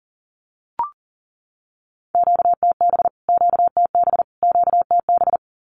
Morse code, heard as a keyed tone, about 40 wpm, 700 Hz. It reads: QTB QTB QTB
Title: Q-Codes Repeated 3x 40wpm